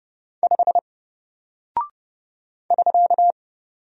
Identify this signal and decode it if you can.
Morse code, heard as a keyed tone, about 30 wpm, 700 Hz. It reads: 5 <SK>
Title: Single Character (Letter:Number:Punctuation:Prosign) Alt 30wpm